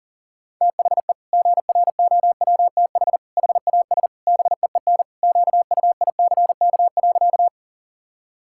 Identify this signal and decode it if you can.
Morse code, heard as a keyed tone, about 40 wpm, 700 Hz. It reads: THE GROWTH HAS BEEN QUICK.
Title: Sentences from Top 1000 Words - Review Part 5 - 40wpm